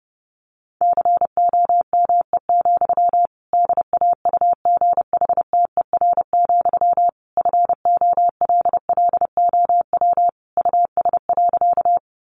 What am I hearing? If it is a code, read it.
Morse code, heard as a keyed tone, about 30 wpm, 700 Hz. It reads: COME, DAUGHTER, FOLLOW US.
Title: Sentences from Top 1500 Words 30wpm